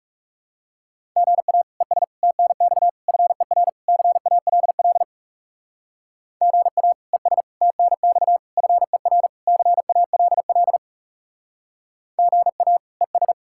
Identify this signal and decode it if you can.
Morse code, heard as a keyed tone, about 36 wpm, 700 Hz. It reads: GA ES TNX FER CALL GA ES TNX FER CALL GA ES
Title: Speed-Racing Common QSO Phrases - 30wpm